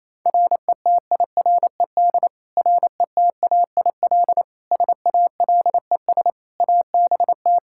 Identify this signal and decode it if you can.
Morse code, heard as a keyed tone, about 28 wpm, 700 Hz. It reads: RETIRED RETAIL SALES ABT
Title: Practice QSO - Exchange 3 - More Info - 28wpm